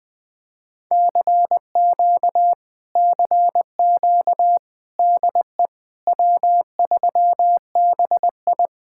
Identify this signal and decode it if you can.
Morse code, heard as a keyed tone, about 20 wpm, 700 Hz. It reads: CQ CQ DE W3BI
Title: Speed-Racing Common QSO Phrases - 17wpm